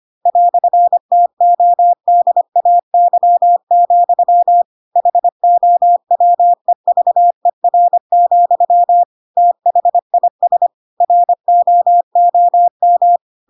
Morse code, heard as a keyed tone, about 25 wpm, 700 Hz. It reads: "TODAY, HOWEVER, THIS ROOM
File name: Pinocchio / Pinocchio ch4 25 25 ik1hge